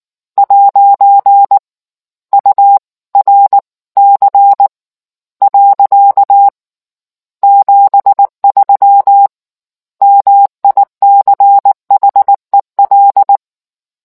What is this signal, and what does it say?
Morse code, heard as a keyed tone, about 19 wpm, spaced as for 15 wpm, 800 Hz. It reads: 'URC. 73 MICHEL